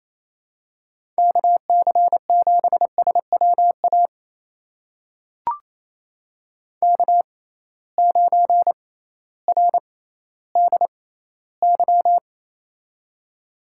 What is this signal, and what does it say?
Morse code, heard as a keyed tone, about 27 wpm, spaced as for 15 wpm, 700 Hz. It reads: KC7SWA K9RDY
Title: USA Call Signs - 6x Character Spacing - 28wpm